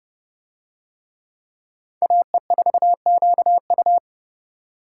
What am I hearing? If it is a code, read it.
Morse code, heard as a keyed tone, about 28 wpm, 700 Hz. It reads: AE4QU